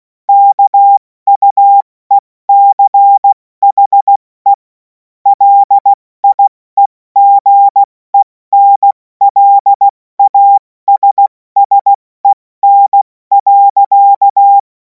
Morse code, heard as a keyed tone, about 16 wpm, 800 Hz. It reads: KUECHE LIEGENLASSEN.
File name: Morsekurs / Witze / Witz 10. März 2026 70 F80